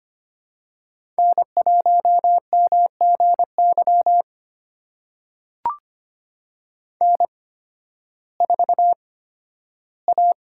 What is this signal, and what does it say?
Morse code, heard as a keyed tone, about 25 wpm, spaced as for 13 wpm, 700 Hz. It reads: N1MGY N4A
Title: USA Call Signs - 5 Characters - 8x Character Spacing - 25wpm